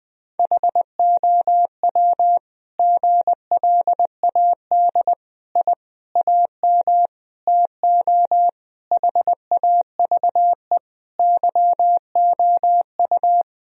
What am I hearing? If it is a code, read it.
Morse code, heard as a keyed tone, about 21 wpm, 700 Hz. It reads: HOW GLAD I AM TO HAVE YOU
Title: Sentences from Top 1000 Words - Review Part 2 - 20wpm